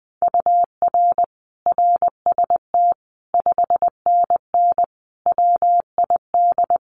Morse code, heard as a keyed tone, about 20 wpm, 700 Hz. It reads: UR RST 5NN WID
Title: Common QSO Phrases 20wpm